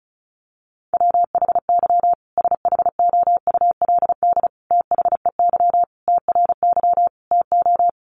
Morse code, heard as a keyed tone, about 34 wpm, 700 Hz. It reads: WHY SHOULD THEY TRY TO
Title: Sentences from Top 1100 Words 35wpm